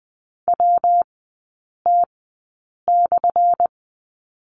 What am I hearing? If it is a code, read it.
Morse code, heard as a keyed tone, about 20 wpm, spaced as for 8 wpm, 700 Hz. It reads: WT/
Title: 205 Lesson 33e - Letter j - Three Characters 20wpm